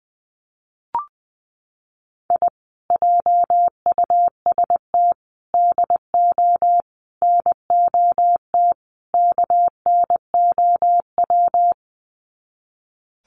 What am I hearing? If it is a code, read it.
Morse code, heard as a keyed tone, about 20 wpm, 700 Hz. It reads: I JUST DO NOT KNOW
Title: Sets of 5 Words Encore 20wpm